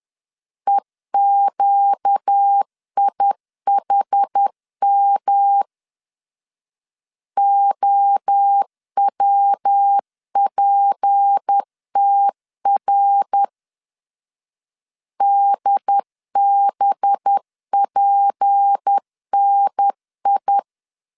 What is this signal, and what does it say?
Morse code, heard as a keyed tone, about 10 wpm, 800 Hz. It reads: EQIHM OWPTR DBPNI
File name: LEZIONI / 6 PKX0 / 6 P 10wpm 2432